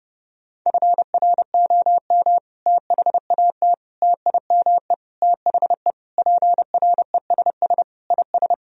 Morse code, heard as a keyed tone, about 30 wpm, 700 Hz. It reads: FROM THAT TIME THE PRESS IS
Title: Sentences from Top 1000 Words - Review Part 1 - 30wpm